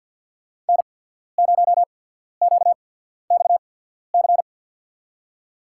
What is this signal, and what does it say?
Morse code, heard as a keyed tone, about 54 wpm, spaced as for 14 wpm, 700 Hz. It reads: N0QXC
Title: USA Call Signs - 5 Characters - 8x Character Spacing - 50wpm